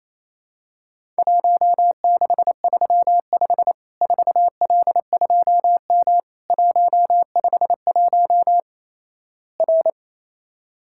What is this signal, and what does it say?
Morse code, heard as a keyed tone, about 28 wpm, 700 Hz. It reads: 1635 4L2M 151 R
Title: Worked All Europe DX Contest - QTC Exchange - 28wpm